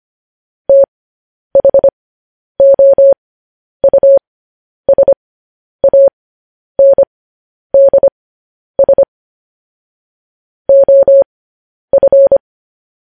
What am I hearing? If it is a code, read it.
Morse code, heard as a keyed tone, about 25 wpm, spaced as for 10 wpm, 550 Hz. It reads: THOUSANDS OF
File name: podcast / News headlines-1x-25x10